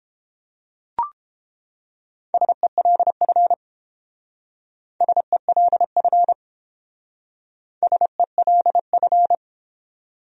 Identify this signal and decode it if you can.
Morse code, heard as a keyed tone, about 29 wpm, 700 Hz. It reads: SELF SELF SELF